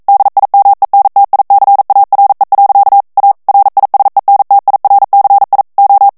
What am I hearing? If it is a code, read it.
Morse code, heard as a keyed tone, about 42 wpm, 800 Hz. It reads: DIMENTICARE. A RISENTIRCI O
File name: morse mp3 / ITALIEN texte clair 1 vitesse 40 mots minute